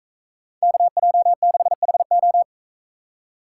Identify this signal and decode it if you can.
Morse code, heard as a keyed tone, about 44 wpm, 700 Hz. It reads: KJ6HO